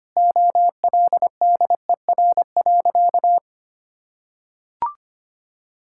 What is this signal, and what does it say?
Morse code, heard as a keyed tone, about 25 wpm, 700 Hz. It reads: OLDER.
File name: practice / 25wpm / Difficult Sentences with Hints - 25wpm